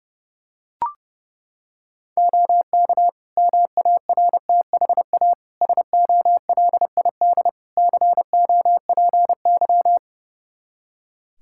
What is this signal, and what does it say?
Morse code, heard as a keyed tone, about 32 wpm, 700 Hz. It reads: OK MARTHA SOLID COPY
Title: Practice QSO - Exchange 3 - More Info - 30wpm